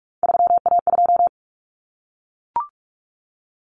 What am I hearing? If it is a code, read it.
Morse code, heard as a keyed tone, about 44 wpm, 700 Hz. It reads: 3A2
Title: Common Cut Numbers - 3-Digit 45wpm